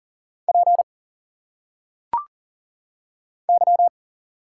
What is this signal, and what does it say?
Morse code, heard as a keyed tone, about 39 wpm, 700 Hz. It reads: P Y
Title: Single Letters - Phonetic Pronunciation 40wpm